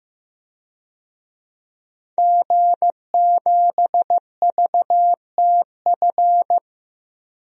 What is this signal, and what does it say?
Morse code, heard as a keyed tone, about 15 wpm, 700 Hz. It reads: G7VTF
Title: DXCC-100 Call Signs - 5 Characters - 15wpm